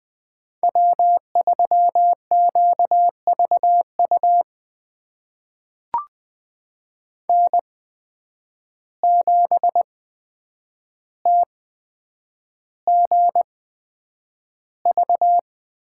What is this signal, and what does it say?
Morse code, heard as a keyed tone, about 20 wpm, spaced as for 9 wpm, 700 Hz. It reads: W3QVU N7TGV
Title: USA Call Signs - 5 Characters - 8x Character Spacing - 20wpm